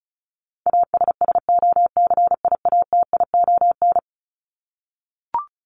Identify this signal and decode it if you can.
Morse code, heard as a keyed tone, about 34 wpm, 700 Hz. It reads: ASSOCIATION
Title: Top 1001-1250 Words 35wpm